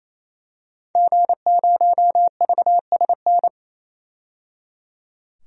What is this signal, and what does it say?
Morse code, heard as a keyed tone, about 28 wpm, 700 Hz. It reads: G0VSN
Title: DXCC-100 Call Signs - 5 Characters - 28wpm